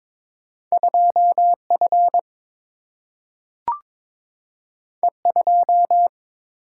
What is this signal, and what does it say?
Morse code, heard as a keyed tone, about 22 wpm, 700 Hz. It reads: 2F E2
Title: Single-Letter Single-Number 22wpm